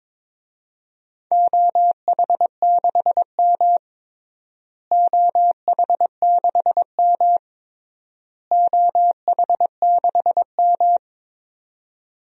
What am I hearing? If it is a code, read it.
Morse code, heard as a keyed tone, about 22 wpm, 700 Hz. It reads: OH6M OH6M OH6M
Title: DXCC-100 Call Signs - 4 Characters - Repeated 3x - 22wpm